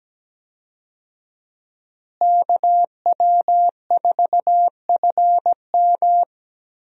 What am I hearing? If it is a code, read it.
Morse code, heard as a keyed tone, about 17 wpm, 700 Hz. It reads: KW4FM
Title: USA Call Signs - 5 Characters - Repeated 3x - 17wpm